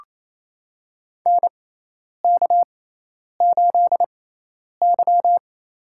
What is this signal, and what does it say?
Morse code, heard as a keyed tone, about 27 wpm, spaced as for 10 wpm, 700 Hz. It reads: NK8Y
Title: USA Call Signs - 4 Characters - 6x Character Spacing - 28wpm